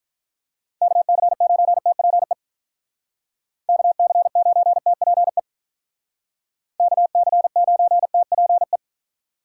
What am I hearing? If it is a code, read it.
Morse code, heard as a keyed tone, about 46 wpm, 700 Hz. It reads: KC9TPE KC9TPE KC9TPE